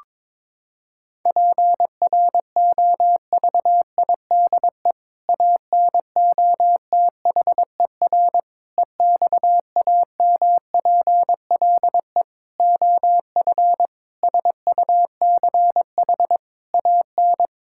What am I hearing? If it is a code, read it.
Morse code, heard as a keyed tone, about 22 wpm, 700 Hz. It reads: PROVIDE ANOTHER EXAMPLE OF SUCH AN
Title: Sentences from Top 1000 Words - Review Part 5 - 22wpm